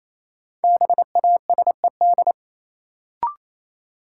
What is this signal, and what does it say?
Morse code, heard as a keyed tone, about 28 wpm, 700 Hz. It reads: BASED
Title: Sets of 2 Words - New Word Drill - 28wpm